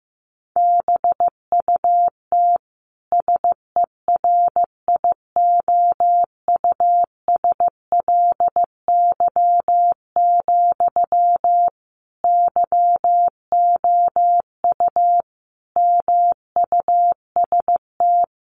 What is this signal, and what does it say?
Morse code, heard as a keyed tone, about 15 wpm, 700 Hz. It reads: BUT SERIOUSLY, YOU MUST
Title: Sentences from Top 1200 Words 15wpm